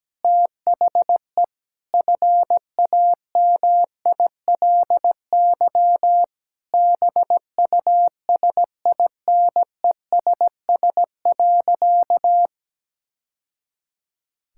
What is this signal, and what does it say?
Morse code, heard as a keyed tone, about 17 wpm, 700 Hz. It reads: THE FAMILY BUSINESS.
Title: Sentences from Top 1000 Words - Review Part 6 - 17wpm